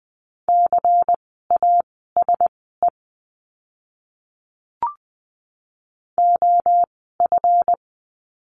Morse code, heard as a keyed tone, about 20 wpm, spaced as for 15 wpm, 700 Hz. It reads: CASE OF
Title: 146 Lesson 24f - Letter y - Words 20wpm